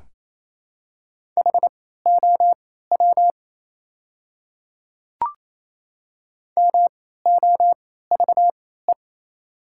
Morse code, heard as a keyed tone, about 28 wpm, spaced as for 16 wpm, 690 Hz. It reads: HOW MOVE
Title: 3x Character Spacing - Top 300 Words - 28wpm